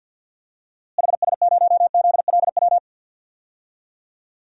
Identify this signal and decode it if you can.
Morse code, heard as a keyed tone, about 48 wpm, 700 Hz. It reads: HS0ZLW
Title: DXCC-100 Call Signs - 6 Characters - Repeated 2x - 50wpm